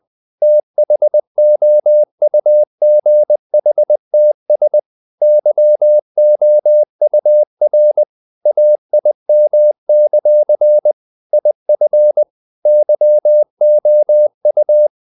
Morse code, heard as a keyed tone, about 20 wpm, 600 Hz. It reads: THOUGHTS YOUR AIM; IF YOU
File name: CW / Practice / If20